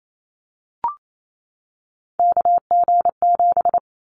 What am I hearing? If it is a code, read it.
Morse code, heard as a keyed tone, about 28 wpm, 700 Hz. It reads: KG7
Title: Callsign Prefixes for Top 25 DXCC - 28wpm